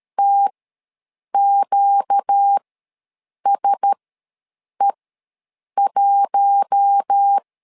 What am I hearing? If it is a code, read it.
Morse code, heard as a keyed tone, about 13 wpm, spaced as for 7 wpm, 800 Hz. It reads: TQSE1